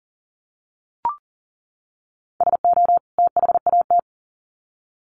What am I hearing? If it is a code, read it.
Morse code, heard as a keyed tone, about 40 wpm, 700 Hz. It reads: SO THAT
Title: Sets of 2 Words 40wpm